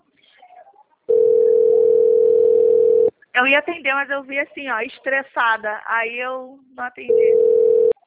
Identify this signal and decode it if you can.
Ringback tone (North America), 440 + 480 Hz, with speech.
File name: out- - - - - .41009